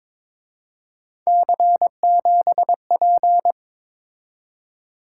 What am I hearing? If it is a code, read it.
Morse code, heard as a keyed tone, about 26 wpm, 700 Hz. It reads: C7P
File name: Speed-Racing - Three Letter-Number Combinations - 22wpm